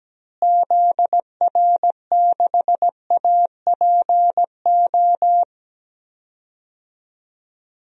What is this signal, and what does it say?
Morse code, heard as a keyed tone, about 17 wpm, 700 Hz. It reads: ZR6APO
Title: DXCC-100 Call Signs - 6 Characters - Repeated 3x - 17wpm